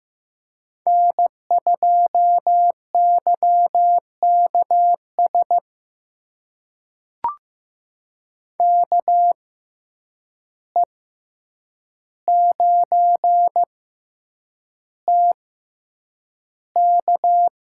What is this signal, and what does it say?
Morse code, heard as a keyed tone, about 15 wpm, spaced as for 8 wpm, 700 Hz. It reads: N2YKS KE9TK
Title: USA Call Signs - 5 Characters - 6x Character Spacing - 15wpm